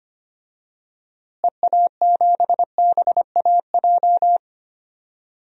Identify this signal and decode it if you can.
Morse code, heard as a keyed tone, about 25 wpm, 700 Hz. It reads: EA7BAJ
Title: DXCC-100 Call Signs - 6 Characters - Repeated 3x - 25wpm